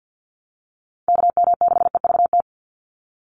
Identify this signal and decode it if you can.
Morse code, heard as a keyed tone, about 50 wpm, 700 Hz. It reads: KM6EVT
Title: USA Call Signs 50wpm